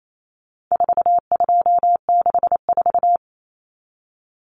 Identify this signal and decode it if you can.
Morse code, heard as a keyed tone, about 28 wpm, 700 Hz. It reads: 4264
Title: Numbers - 4-Digit 28wpm